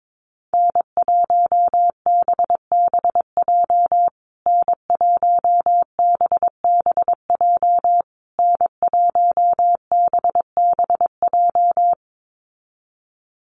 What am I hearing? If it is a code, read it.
Morse code, heard as a keyed tone, about 22 wpm, 700 Hz. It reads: N1BBJ N1BBJ N1BBJ
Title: USA Call Signs Repeated 3x 22wpm